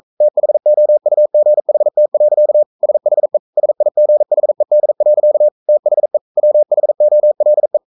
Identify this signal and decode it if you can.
Morse code, heard as a keyed tone, about 42 wpm, 600 Hz. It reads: THOUGHT. SHE SIGHED. THE WHOLE
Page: Romanen Never af Ken Follet på engelsk.